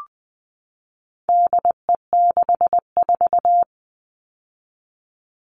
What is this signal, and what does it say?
Morse code, heard as a keyed tone, about 20 wpm, 700 Hz. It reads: DE64
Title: All Cut Numbers - 4-Digit 20wpm